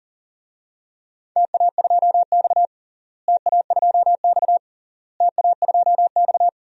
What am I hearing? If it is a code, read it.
Morse code, heard as a keyed tone, about 40 wpm, 700 Hz. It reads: TA2X TA2X TA2X
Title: DXCC-100 Call Signs - 4 Characters - Repeated 3x - 40wpm